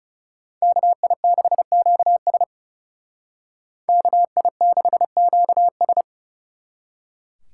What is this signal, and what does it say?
Morse code, heard as a keyed tone, about 34 wpm, 700 Hz. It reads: KI6QS KI6QS
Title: Speed Racing - USA 2x2 Call Signs 30wpm